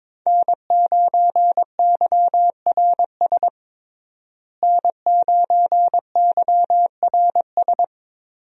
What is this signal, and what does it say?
Morse code, heard as a keyed tone, about 22 wpm, 700 Hz. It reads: N9YRS N9YRS
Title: USA Call Signs - 5 Characters - Repeated 3x - 22wpm